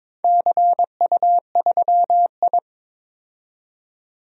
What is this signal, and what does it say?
Morse code, heard as a keyed tone, about 22 wpm, 700 Hz. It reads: CU3I
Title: DXCC-100 Call Signs - 4 Characters - 22wpm